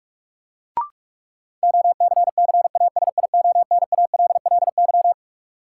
Morse code, heard as a keyed tone, about 46 wpm, 700 Hz. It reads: OCCASIONALLY
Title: Sentences from Top 1300 Words - New Word Drill - 45wpm